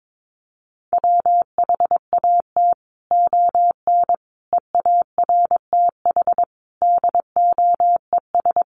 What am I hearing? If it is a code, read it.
Morse code, heard as a keyed tone, about 22 wpm, 700 Hz. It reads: WHAT ON EARTH DOES
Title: Sentences from Top 200 Words 22wpm